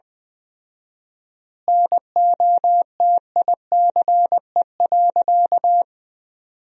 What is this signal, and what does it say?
Morse code, heard as a keyed tone, about 21 wpm, 700 Hz. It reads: NOTICE.
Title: 4x Word-Spacing - Sentences from Top 400 Words - 20wpm